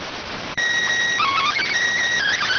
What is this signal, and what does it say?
SSTV (slow-scan television), PD 290.